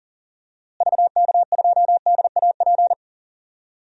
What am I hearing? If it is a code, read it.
Morse code, heard as a keyed tone, about 40 wpm, 700 Hz. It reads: VK2DAP
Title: DXCC-100 Call Signs - 6 Characters - Repeated 3x - 40wpm